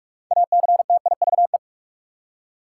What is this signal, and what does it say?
Morse code, heard as a keyed tone, about 46 wpm, 700 Hz. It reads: ACTIVE
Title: Six Letter Words - 45wpm